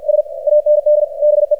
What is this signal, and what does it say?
Morse code, heard as a keyed tone, about 48 wpm, 600 Hz, with radio noise. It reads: S AND BE
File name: archived / CRSnet